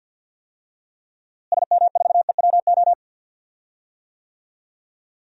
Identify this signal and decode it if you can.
Morse code, heard as a keyed tone, about 53 wpm, 700 Hz. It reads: SM4EWO
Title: DXCC-100 Call Signs - 6 Characters - Repeated 3x - 50wpm